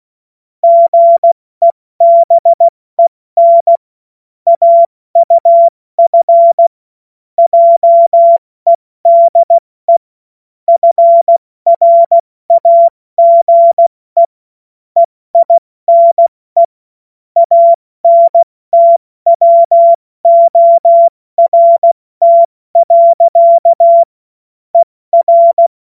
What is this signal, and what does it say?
Morse code, heard as a keyed tone, about 16 wpm, 680 Hz. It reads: GEBEN AUF JEDE FRAGE EINE ANTWORT. ER
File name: Morsekurs / Witze / Witz 09 Januar 2024 70 F80